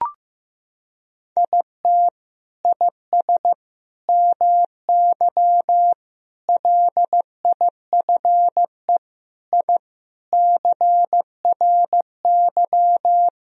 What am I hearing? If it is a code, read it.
Morse code, heard as a keyed tone, about 15 wpm, 700 Hz. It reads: IT IS MY LIFE I CRY
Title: Sentences from Top 1000 Words - Review Part 2 - 15wpm